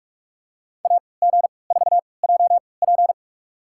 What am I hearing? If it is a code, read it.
Morse code, heard as a keyed tone, about 45 wpm, spaced as for 25 wpm, 700 Hz. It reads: AG4JP
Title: USA Call Signs - 5 Characters - 3x Character Spacing - 45wpm